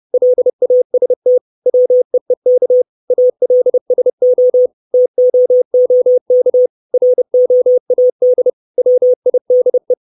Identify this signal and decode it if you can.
Morse code, heard as a keyed tone, about 30 wpm, 500 Hz. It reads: LAST WEEK ALSO TOOK ROAD WIDE
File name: 632 Words 30WPM